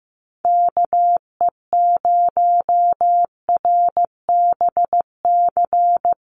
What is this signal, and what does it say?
Morse code, heard as a keyed tone, about 15 wpm, 700 Hz. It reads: KE0RBC
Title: USA Call Signs Repeated 2x 15wpm